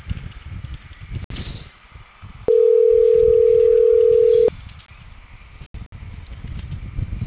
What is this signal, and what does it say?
Ringback tone (North America), 440 + 480 Hz.